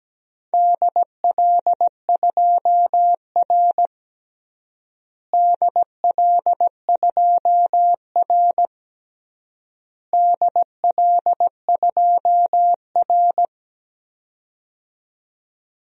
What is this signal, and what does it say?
Morse code, heard as a keyed tone, about 17 wpm, 700 Hz. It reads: DL2R DL2R DL2R
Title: DXCC-100 Call Signs - 4 Characters - Repeated 3x - 17wpm